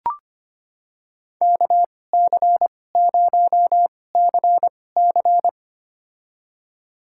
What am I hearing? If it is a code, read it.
Morse code, heard as a keyed tone, about 25 wpm, spaced as for 18 wpm, 700 Hz. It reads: KC0CC
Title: USA Call Signs - 5 Characters - 2x Character Spacing - 25wpm